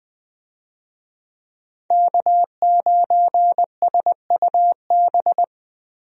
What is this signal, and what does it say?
Morse code, heard as a keyed tone, about 20 wpm, 700 Hz. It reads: K9SUB
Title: USA Call Signs - 5 Characters - Repeated 2x - 20wpm